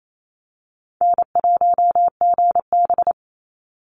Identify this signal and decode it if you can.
Morse code, heard as a keyed tone, about 28 wpm, 700 Hz. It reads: N1GB
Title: NA SOTA Chaser Callsigns 28wpm